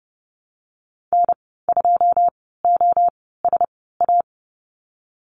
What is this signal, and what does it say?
Morse code, heard as a keyed tone, about 30 wpm, spaced as for 17 wpm, 700 Hz. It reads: N2OSA
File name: Lesson 15g - Number 5 - Call Signs 30wpm